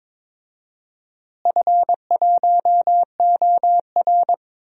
Morse code, heard as a keyed tone, about 22 wpm, 700 Hz. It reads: F1OR
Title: DXCC-100 Call Signs - 4 Characters - Repeated 2x - 22wpm